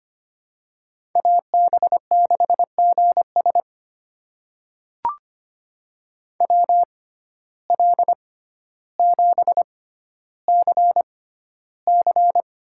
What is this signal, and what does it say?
Morse code, heard as a keyed tone, about 25 wpm, spaced as for 13 wpm, 700 Hz. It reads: AB6GS WL7CC
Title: USA Call Signs - 6x Character Spacing - 25wpm